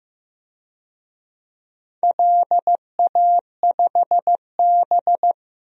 Morse code, heard as a keyed tone, about 16 wpm, 700 Hz. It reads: LA5B
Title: DXCC-100 Call Signs - 4 Characters - Repeated 2x - 15wpm